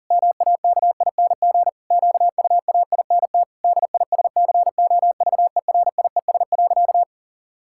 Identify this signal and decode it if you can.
Morse code, heard as a keyed tone, about 40 wpm, 700 Hz. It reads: MAKING QUAINT DISCOVERIES.
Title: Pangram Challenge v2 - 40wpm